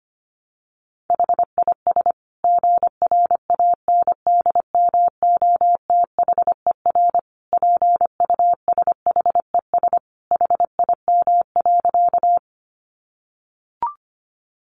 Morse code, heard as a keyed tone, about 25 wpm, 700 Hz. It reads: HIS GRANDMOTHER PUSHES HIM.
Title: Additional Easy Phrases 25wpm